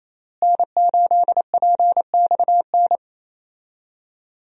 Morse code, heard as a keyed tone, about 28 wpm, 700 Hz. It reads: N8PXN
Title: USA Call Signs - 5 Characters - Repeated 3x - 28wpm